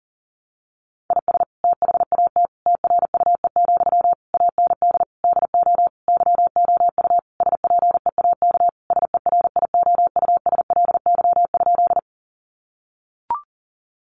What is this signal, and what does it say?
Morse code, heard as a keyed tone, about 40 wpm, 700 Hz. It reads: IS THAT TRUE, AND DO YOU SPEAK SERIOUSLY?
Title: Sentences from Top 1200 Words 40wpm